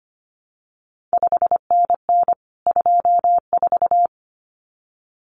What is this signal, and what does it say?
Morse code, heard as a keyed tone, about 25 wpm, 700 Hz. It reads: 5NN 24
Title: CQ WW DX Contest Exchange Encore - 25wpm